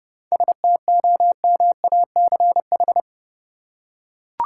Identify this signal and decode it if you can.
Morse code, heard as a keyed tone, about 30 wpm, 700 Hz. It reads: STOMACH
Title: Seven Letter Words - 30wpm